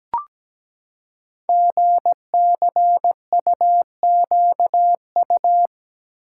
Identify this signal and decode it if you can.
Morse code, heard as a keyed tone, about 17 wpm, 700 Hz. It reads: GCUQU
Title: Code Groups - 5 Letters - 17wpm